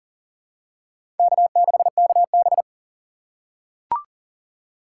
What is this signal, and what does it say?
Morse code, heard as a keyed tone, about 40 wpm, 700 Hz. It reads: K6KB
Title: USA Call Signs - 4 Characters - 3x Character Spacing - 40wpm